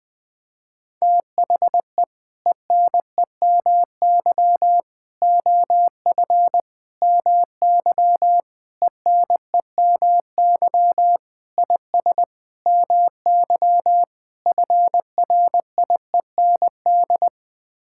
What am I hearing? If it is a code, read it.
Morse code, heard as a keyed tone, about 20 wpm, 700 Hz. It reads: THE ENEMY OF MY ENEMY IS MY FRIEND
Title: Sets of 8 Words - 20wpm